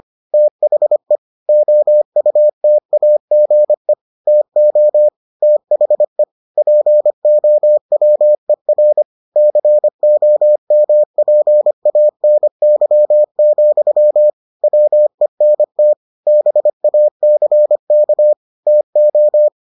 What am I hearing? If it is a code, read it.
Morse code, heard as a keyed tone, about 25 wpm, 600 Hz. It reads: THE OUTAGE TO THE POWER COMPANY, WENT BACK TO